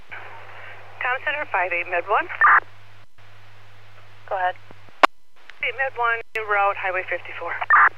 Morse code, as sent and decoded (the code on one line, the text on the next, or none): none